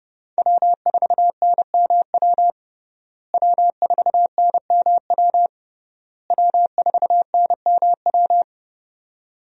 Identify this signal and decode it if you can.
Morse code, heard as a keyed tone, about 30 wpm, 700 Hz. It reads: W4NMW W4NMW W4NMW
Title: USA Call Signs - 5 Characters - Repeated 3x - 30wpm